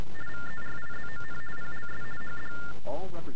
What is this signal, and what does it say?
TTY/TDD text telephone tones, about 50 baud, with speech. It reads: VERIZON PLS HD